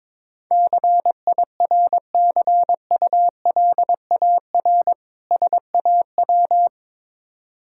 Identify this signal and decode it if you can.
Morse code, heard as a keyed tone, about 22 wpm, 690 Hz. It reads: CIRCULAR SAW
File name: Difficult Words with Hints - 22wpm